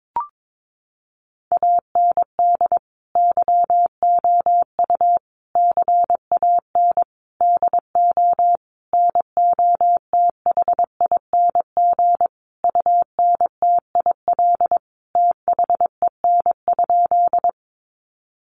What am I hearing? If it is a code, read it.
Morse code, heard as a keyed tone, about 22 wpm, 700 Hz. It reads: AND YOU CAN DO NOTHING UNTIL THEN?
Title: Sentences from Top 500 Words 22wpm